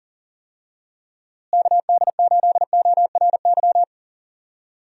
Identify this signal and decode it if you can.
Morse code, heard as a keyed tone, about 40 wpm, 700 Hz. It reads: KD8ORY
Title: USA Call Signs - 6 Characters - Repeated 2x - 40wpm